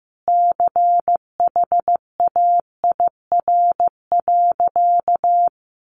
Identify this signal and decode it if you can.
Morse code, heard as a keyed tone, about 15 wpm, 700 Hz. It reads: CHAIR.
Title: A Sound Of Thunder 15wpm